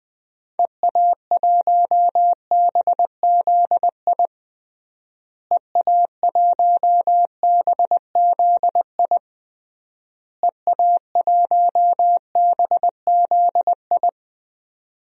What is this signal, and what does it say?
Morse code, heard as a keyed tone, about 20 wpm, 700 Hz. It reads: EA1BZI EA1BZI EA1BZI
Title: DXCC-100 Call Signs - 6 Characters - Repeated 3x - 20wpm